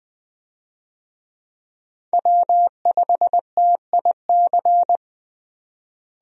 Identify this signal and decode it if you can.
Morse code, heard as a keyed tone, about 20 wpm, 700 Hz. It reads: W5TIC
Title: USA Call Signs - 5 Characters - Repeated 2x - 20wpm